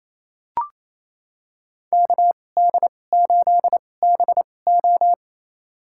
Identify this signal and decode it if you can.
Morse code, heard as a keyed tone, about 28 wpm, spaced as for 20 wpm, 700 Hz. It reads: KD8BO
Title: USA Call Signs - 2x Character Spacing - 28wpm